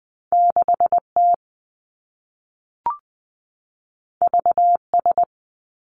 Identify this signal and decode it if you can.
Morse code, heard as a keyed tone, about 20 wpm, 700 Hz. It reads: 6 T V S
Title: Two Challenging Letter-Number Combinations 20wpm